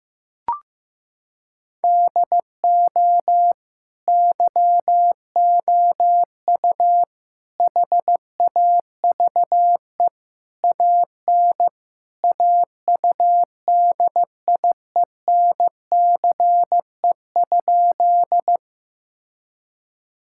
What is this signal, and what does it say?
Morse code, heard as a keyed tone, about 15 wpm, 700 Hz. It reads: DO YOU HAVE AN AUDIENCE?
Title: Sentences from Top 1000 Words - Review Part 5 - 15wpm